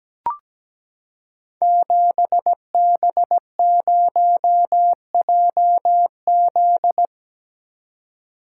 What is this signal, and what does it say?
Morse code, heard as a keyed tone, about 17 wpm, 700 Hz. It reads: 7B0JZ
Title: Code Groups - 5 Letter:Numbers - 17wpm